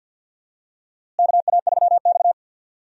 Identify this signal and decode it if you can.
Morse code, heard as a keyed tone, about 52 wpm, 700 Hz. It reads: KA3X
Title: USA Call Signs - 4 Characters - Repeated 3x - 50wpm